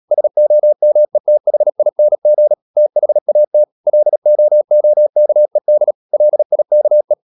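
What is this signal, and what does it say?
Morse code, heard as a keyed tone, about 37 wpm, 600 Hz. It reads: SOMETHING THAT LOOKED LIKE
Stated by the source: Morse code practice at 35–39 wpm sending English text of Ken Follett's novel Never, from CW med Gnister.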